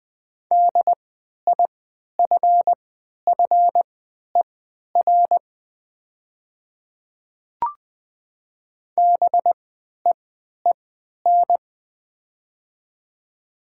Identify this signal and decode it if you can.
Morse code, heard as a keyed tone, about 20 wpm, spaced as for 11 wpm, 700 Hz. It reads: DIFFER BEEN
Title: 3x Character Spacing - Top 200 Words - 20wpm